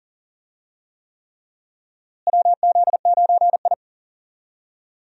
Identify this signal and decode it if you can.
Morse code, heard as a keyed tone, about 40 wpm, 700 Hz. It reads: WZ9I
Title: USA Call Signs - 4 Characters - Repeated 3x - 40wpm